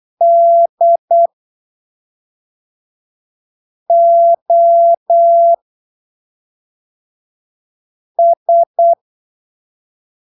Morse code, heard as a keyed tone, about 8 wpm, spaced as for 3 wpm, 680 Hz. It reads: DOS
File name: arquivos / texto / texto 14 08ppm